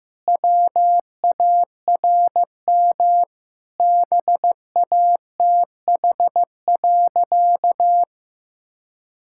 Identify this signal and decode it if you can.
Morse code, heard as a keyed tone, about 16 wpm, 700 Hz. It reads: WARM BATH.
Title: Sentences from Top 1600 Words 15wpm